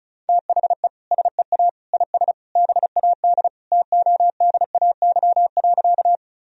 Morse code, heard as a keyed tone, about 36 wpm, 690 Hz. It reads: THE SEA IS BAD TODAY.
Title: Sentences from Top 1000 Words - Review Part 1 - 35wpm